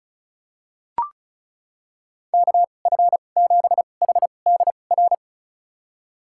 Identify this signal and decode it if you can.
Morse code, heard as a keyed tone, about 34 wpm, spaced as for 25 wpm, 700 Hz. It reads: KF7HDR
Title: USA Call Signs - 2x Character Spacing - 35wpm